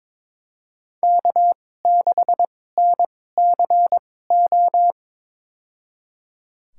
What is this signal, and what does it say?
Morse code, heard as a keyed tone, about 21 wpm, spaced as for 16 wpm, 700 Hz. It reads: K6NCO
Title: USA Call Signs - 5 Characters - 2x Character Spacing - 22wpm